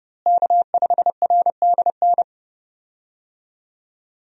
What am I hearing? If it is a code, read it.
Morse code, heard as a keyed tone, about 32 wpm, 700 Hz. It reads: K5RDN
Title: USA Call Signs - 5 Characters - Repeated 2x - 30wpm